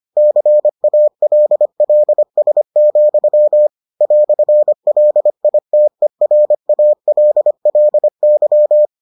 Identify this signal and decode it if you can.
Morse code, heard as a keyed tone, about 25 wpm, 600 Hz. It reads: CALLS, "LITERALLY